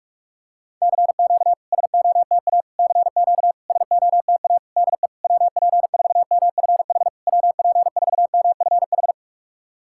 Morse code, heard as a keyed tone, about 46 wpm, 690 Hz. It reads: CQ SOTA CQ SOTA DE WP4MFH WP4MFH
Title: SOTA QSO Practice 45wpm